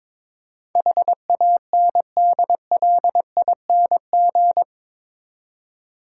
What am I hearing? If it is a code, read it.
Morse code, heard as a keyed tone, about 22 wpm, 700 Hz. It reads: HANDLING
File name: practice / 22wpm / Eight Letter Words - 22wpm